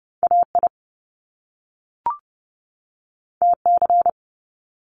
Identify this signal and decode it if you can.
Morse code, heard as a keyed tone, about 30 wpm, 700 Hz. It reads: AI TC
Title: Top Two Letter Combinations in Words 30wpm